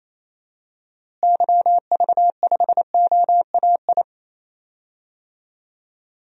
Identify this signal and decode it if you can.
Morse code, heard as a keyed tone, about 27 wpm, 700 Hz. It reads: YV5OAI
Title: DXCC-100 Call Signs - 6 Characters - Repeated 3x - 28wpm